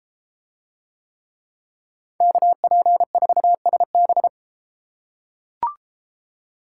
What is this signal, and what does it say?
Morse code, heard as a keyed tone, about 33 wpm, 700 Hz. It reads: KP4SB